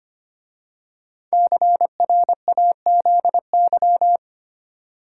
Morse code, heard as a keyed tone, about 25 wpm, 700 Hz. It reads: CRAZY
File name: 5 Letter Words - First 3 Letters Given - 25wpm